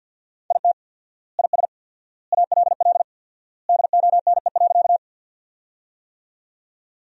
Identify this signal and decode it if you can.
Morse code, heard as a keyed tone, about 47 wpm, 700 Hz. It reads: IT IS ALL DONE.
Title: 4x Word-Spacing - Sentences from Top 500 Words - 50wpm